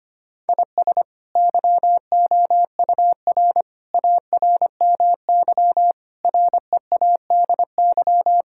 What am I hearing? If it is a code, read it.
Morse code, heard as a keyed tone, about 25 wpm, 700 Hz. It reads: IS YOUR ARMY READY
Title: Sentences from Top 1000 Words - Review Part 5 - 25wpm